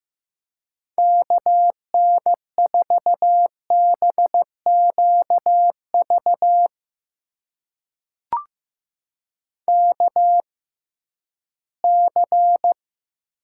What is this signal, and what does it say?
Morse code, heard as a keyed tone, about 15 wpm, spaced as for 11 wpm, 700 Hz. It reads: KN4BQV KC
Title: USA Call Signs - 6 Characters - 6x Character Spacing - 15wpm